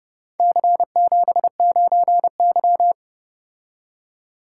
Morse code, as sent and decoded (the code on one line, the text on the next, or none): -.-. --... ----. -.--
C79Y